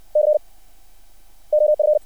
Morse code, heard as a keyed tone, about 60 wpm, 600 Hz. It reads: C OF
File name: archived / CRSnet